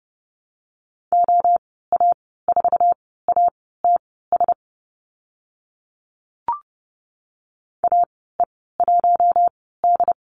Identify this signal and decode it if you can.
Morse code, heard as a keyed tone, about 30 wpm, spaced as for 17 wpm, 700 Hz. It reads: OA4ATS AE1D
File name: Lesson 14g - Number 2 - Call Signs 30wpm